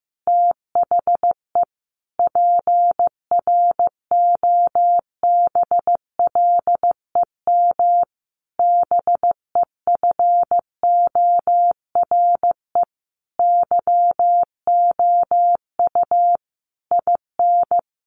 Morse code, heard as a keyed tone, about 15 wpm, 700 Hz. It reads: THE PROBLEM BEFORE YOU IN
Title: Sentences from Top 1500 Words 15wpm